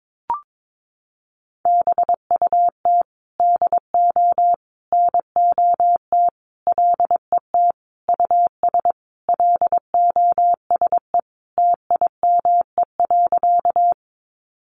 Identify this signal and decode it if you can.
Morse code, heard as a keyed tone, about 22 wpm, 700 Hz. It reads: BUT DO NOT LET US LOSE TIME.
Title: Sentences from Top 1100 Words 22wpm